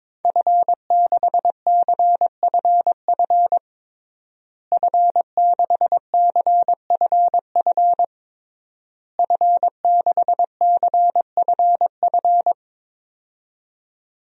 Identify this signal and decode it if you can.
Morse code, heard as a keyed tone, about 22 wpm, 700 Hz. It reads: F6CFF F6CFF F6CFF
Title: DXCC-100 Call Signs - 5 Characters - Repeated 3x - 22wpm